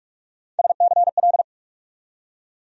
Morse code, heard as a keyed tone, about 43 wpm, 700 Hz. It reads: SCL
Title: USA Call Sign Postfixes - 3 Letters 45wpm